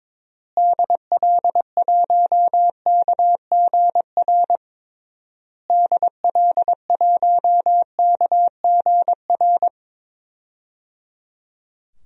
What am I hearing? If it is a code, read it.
Morse code, heard as a keyed tone, about 22 wpm, 700 Hz. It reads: DL1KGR DL1KGR
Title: DXCC-100 Call Signs - 6 Characters - Repeated 3x - 22wpm